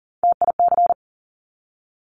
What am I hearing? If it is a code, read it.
Morse code, heard as a keyed tone, about 39 wpm, 700 Hz. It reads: TIC
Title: Top Three Letter Combinations in Words 40wpm